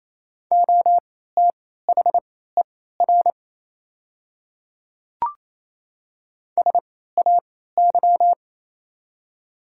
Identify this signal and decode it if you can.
Morse code, heard as a keyed tone, about 28 wpm, spaced as for 16 wpm, 700 Hz. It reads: OTHER SAY
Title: 3x Character Spacing - Top 300 Words - 28wpm